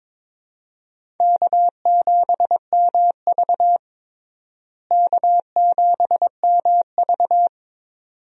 Morse code, heard as a keyed tone, about 22 wpm, 700 Hz. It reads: K7MV K7MV
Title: USA Call Signs - 4 Characters - Repeated 2x - 22wpm